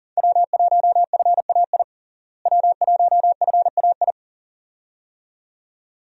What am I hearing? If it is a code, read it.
Morse code, heard as a keyed tone, about 40 wpm, 700 Hz. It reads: W1FAI W1FAI
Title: USA Call Signs - 5 Characters - Repeated 2x - 40wpm